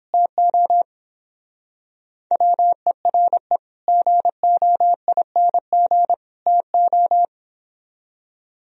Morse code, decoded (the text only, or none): TO WERE GOING TO